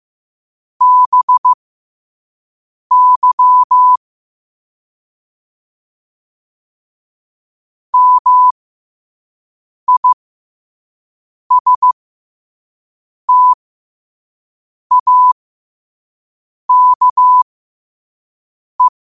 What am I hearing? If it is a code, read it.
Morse code, heard as a keyed tone, about 15 wpm, spaced as for 5 wpm, 1000 Hz. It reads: BY MISTAKE
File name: qotdfiles / 5wpm